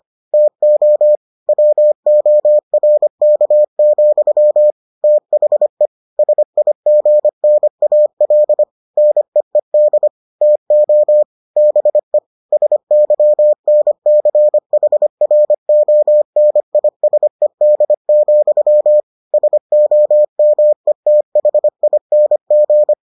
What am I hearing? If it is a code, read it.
Morse code, heard as a keyed tone, about 25 wpm, 600 Hz. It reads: TO WORK, THE SIGNAL NEED TO BE SYNCHRONISED, SOMETHING